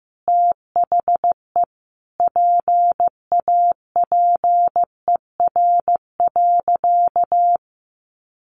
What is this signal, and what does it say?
Morse code, heard as a keyed tone, about 15 wpm, 700 Hz. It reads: THE PAPER.
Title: Sentences from Top 1300 Words 15wpm